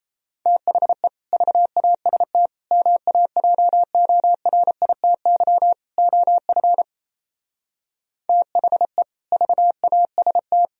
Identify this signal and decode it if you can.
Morse code, heard as a keyed tone, about 33 wpm, 700 Hz. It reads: THE VAST MAJORITY OF THE VAST
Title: Speed-Racing - Sets of 4 Words 28wpm